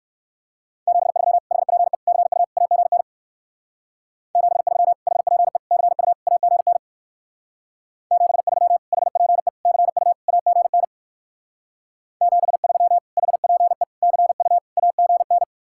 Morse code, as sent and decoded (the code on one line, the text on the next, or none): --... ...-- / .... .--. / -.-. ..- / .- --. -. / --... ...-- / .... .--. / -.-. ..- / .- --. -. / --... ...-- / .... .--. / -.-. ..- / .- --. -. / --... ...-- / .... .--. / -.-. ..- / .- --. -.
73 HPE CU AGN 73 HPE CU AGN 73 HPE CU AGN 73 HPE CU AGN